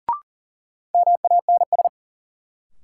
Morse code, heard as a keyed tone, about 40 wpm, 700 Hz. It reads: MANS
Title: Sentences from Top 1000 Words - Review Part 6 - New Word Drill - 40wpm